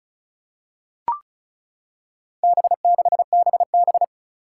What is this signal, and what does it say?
Morse code, heard as a keyed tone, about 36 wpm, 700 Hz. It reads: B6BB
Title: Four Challenging Letter-Number Combinations 35wpm